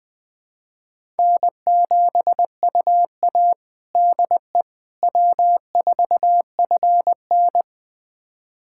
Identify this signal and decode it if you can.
Morse code, heard as a keyed tone, about 20 wpm, 700 Hz. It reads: N7UA DE W4FN
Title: USA Callsign DE Callsign - 4 Characters - 20wpm